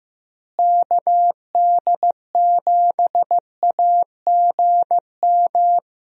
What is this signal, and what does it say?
Morse code, heard as a keyed tone, about 15 wpm, 700 Hz. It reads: KD7AGM
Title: USA Call Signs - 6 Characters - Repeated 2x - 15wpm